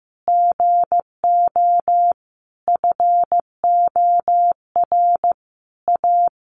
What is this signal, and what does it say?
Morse code, heard as a keyed tone, about 15 wpm, 700 Hz. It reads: GO FOR A
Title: Sentences from Top 1000 Words 15wpm